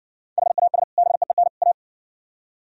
Morse code, heard as a keyed tone, about 59 wpm, 700 Hz. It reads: HAS BEEN A